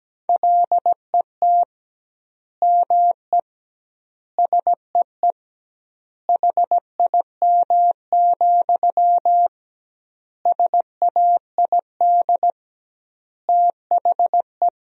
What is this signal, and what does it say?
Morse code, heard as a keyed tone, about 17 wpm, 700 Hz. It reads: LET ME SEE HIM, SAID THE
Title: 2x Word-Spacing - Sentences from Top 400 Words - 17wpm